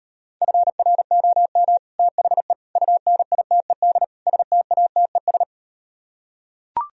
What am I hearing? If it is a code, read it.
Morse code, heard as a keyed tone, about 38 wpm, 700 Hz. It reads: FROM THE UNITED STATES